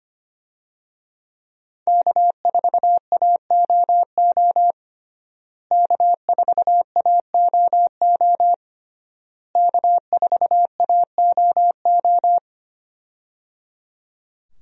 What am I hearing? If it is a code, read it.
Morse code, heard as a keyed tone, about 25 wpm, 700 Hz. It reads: K4AOO K4AOO K4AOO
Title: USA Call Signs - 5 Characters - Repeated 3x - 25wpm